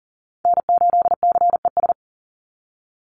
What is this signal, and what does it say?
Morse code, heard as a keyed tone, about 40 wpm, 690 Hz. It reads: N8CES
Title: USA Call Signs 40wpm